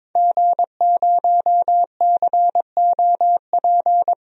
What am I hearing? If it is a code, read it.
Morse code, heard as a keyed tone, about 22 wpm, 700 Hz. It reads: G0COP
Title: DXCC-100 Call Signs - 5 Characters - Repeated 3x - 22wpm